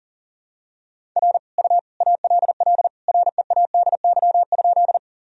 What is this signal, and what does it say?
Morse code, heard as a keyed tone, about 40 wpm, 700 Hz. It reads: R U ALL READY?
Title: Sentences from Top 400 Words and Abbreviations 40wpm